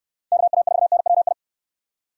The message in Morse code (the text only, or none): KA4NWI